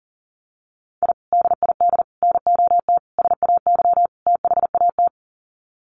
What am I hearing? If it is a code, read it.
Morse code, heard as a keyed tone, about 40 wpm, 700 Hz. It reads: I DID NOT SAY THAT
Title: Sets of 5 Words Encore2 40wpm